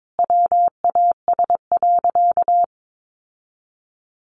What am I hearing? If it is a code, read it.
Morse code, heard as a keyed tone, about 22 wpm, 700 Hz. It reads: WAS.
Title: Sentences from Top 800 Words 22wpm